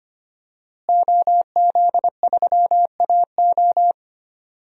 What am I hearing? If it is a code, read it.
Morse code, heard as a keyed tone, about 25 wpm, 700 Hz. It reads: OZ3AO
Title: DXCC-100 Call Signs - 5 Characters - Repeated 2x - 25wpm